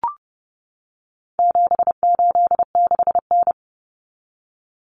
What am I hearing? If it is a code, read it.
Morse code, heard as a keyed tone, about 30 wpm, 700 Hz. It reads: 786N
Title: Common Cut Numbers - 4-Digit 30wpm